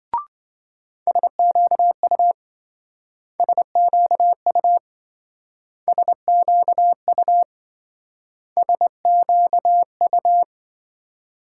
Morse code, heard as a keyed tone, about 27 wpm, 700 Hz. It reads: SQU SQU SQU SQU